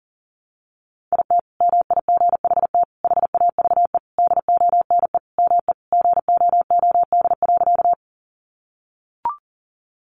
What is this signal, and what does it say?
Morse code, heard as a keyed tone, about 37 wpm, 700 Hz. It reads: IT MIGHT HAVE DONE ME GOOD.
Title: Sentences from Top 500 Words 40wpm